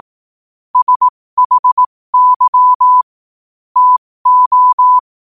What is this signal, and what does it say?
Morse code, heard as a keyed tone, about 18 wpm, spaced as for 15 wpm, 1000 Hz. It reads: SHY TO
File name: qotdfiles / 15wpm